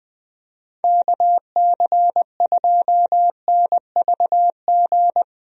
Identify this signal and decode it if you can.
Morse code, heard as a keyed tone, about 21 wpm, 700 Hz. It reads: KC2NVG
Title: USA Call Signs - 6 Characters - 20wpm